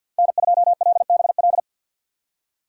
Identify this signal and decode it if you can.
Morse code, heard as a keyed tone, about 49 wpm, 700 Hz. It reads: N2LBL
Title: USA Call Signs - 5 Characters - 4x Character Spacing - 50wpm